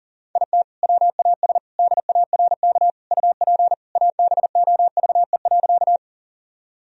Morse code, heard as a keyed tone, about 40 wpm, 700 Hz. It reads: IT WAS DARK UP ABOVE.
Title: Sentences from Top 1000 Words - Review Part 2 - 40wpm